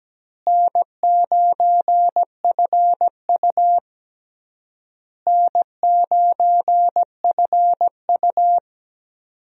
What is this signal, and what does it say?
Morse code, heard as a keyed tone, about 17 wpm, 700 Hz. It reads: N9FU N9FU
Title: USA Call Signs - 4 Characters - Repeated 3x - 17wpm